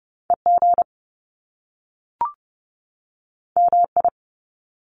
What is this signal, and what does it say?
Morse code, heard as a keyed tone, about 30 wpm, 700 Hz. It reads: E G M I
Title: Top Two Letter Combinations in Words 30wpm